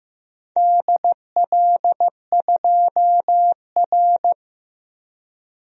Morse code, heard as a keyed tone, about 15 wpm, 700 Hz. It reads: DL2R
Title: DXCC-100 Call Signs - 4 Characters - 15wpm